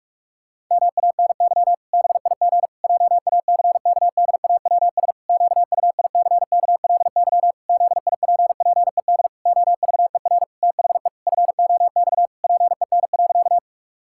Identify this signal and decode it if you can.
Morse code, heard as a keyed tone, about 46 wpm, 700 Hz. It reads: MANY BIG JACKDAWS QUICKLY ZIPPED OVER THE FOX PEN.
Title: Pangram Challenge v2 - 45wpm